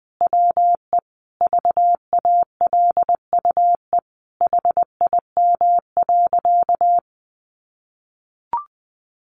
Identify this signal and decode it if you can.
Morse code, heard as a keyed tone, about 20 wpm, 700 Hz. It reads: WE VALUE HIM.
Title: Sentences from Top 600 Words 20wpm